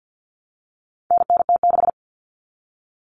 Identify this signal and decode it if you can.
Morse code, heard as a keyed tone, about 50 wpm, 700 Hz. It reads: NNT6
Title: Common Cut Numbers - 4-Digit 50wpm